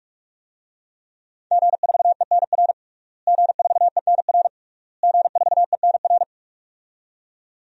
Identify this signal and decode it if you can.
Morse code, heard as a keyed tone, about 46 wpm, 700 Hz. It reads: G4ENR G4ENR G4ENR
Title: DXCC-100 Call Signs - 5 Characters - Repeated 3x - 45wpm